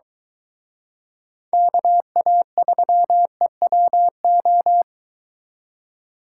Morse code, heard as a keyed tone, about 23 wpm, 700 Hz. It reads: KA3EWO